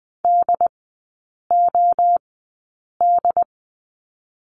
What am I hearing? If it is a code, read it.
Morse code, heard as a keyed tone, about 20 wpm, spaced as for 8 wpm, 700 Hz. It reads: DOD
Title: 067 Lesson 12e - Letter d - Three Characters 20wpm